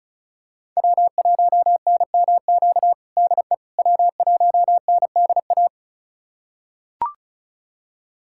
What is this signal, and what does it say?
Morse code, heard as a keyed tone, about 36 wpm, 700 Hz. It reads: W1NMQ DE W1NDA
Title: USA Callsign DE Callsign 35wpm